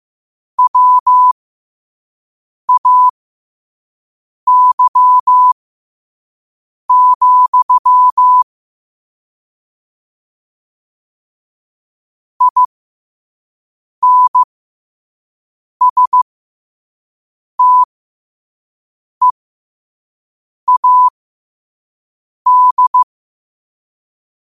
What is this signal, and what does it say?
Morse code, heard as a keyed tone, about 15 wpm, spaced as for 5 wpm, 1000 Hz. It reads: WAY, INSTEAD
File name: qotdfiles / 5wpm